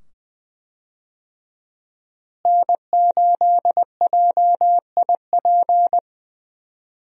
Morse code, heard as a keyed tone, about 20 wpm, 700 Hz. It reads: N8JIP
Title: USA Call Signs - 5 Characters - Repeated 2x - 20wpm